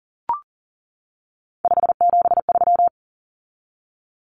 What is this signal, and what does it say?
Morse code, heard as a keyed tone, about 40 wpm, 700 Hz. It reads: 573
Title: Numbers - 3-Digit 40wpm